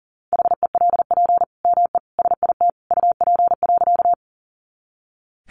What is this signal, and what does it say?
Morse code, heard as a keyed tone, about 40 wpm, 700 Hz. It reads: HELP ME SIT UP.
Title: Sentences from Top 500 Words 40wpm